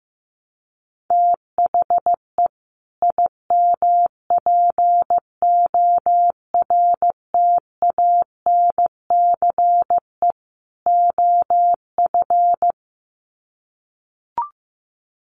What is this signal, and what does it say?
Morse code, heard as a keyed tone, about 15 wpm, 700 Hz. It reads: THE IMPORTANCE OF
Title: Sets of 3 Words 15wpm